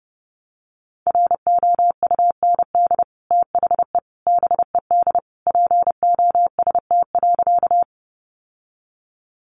Morse code, heard as a keyed tone, about 28 wpm, 700 Hz. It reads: ROUND THE BED POST.
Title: Sentences from Top 1000 Words 30wpm